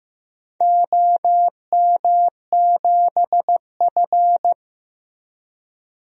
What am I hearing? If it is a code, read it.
Morse code, heard as a keyed tone, about 15 wpm, 700 Hz. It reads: OM7F
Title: DXCC-100 Call Signs - 4 Characters - Repeated 2x - 15wpm